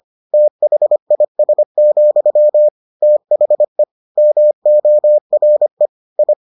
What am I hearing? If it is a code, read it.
Morse code, heard as a keyed tone, about 25 wpm, 600 Hz. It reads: THIS, THE MORE I